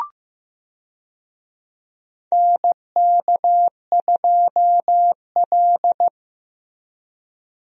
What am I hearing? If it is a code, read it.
Morse code, heard as a keyed tone, about 15 wpm, 700 Hz. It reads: NK2L
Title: USA Call Signs - 4 Characters - Repeated 3x - 15wpm